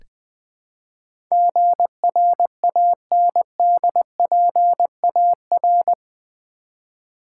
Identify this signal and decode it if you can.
Morse code, heard as a keyed tone, about 20 wpm, 700 Hz. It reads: GRANDPAR
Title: Build A Word - 4 Letters At A Time - 11-Letter Words - 20wpm